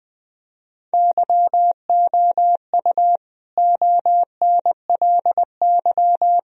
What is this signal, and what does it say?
Morse code, heard as a keyed tone, about 20 wpm, 700 Hz. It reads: YOU ONLY
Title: Sentences from Top 1000 Words - Review Part 2 - 20wpm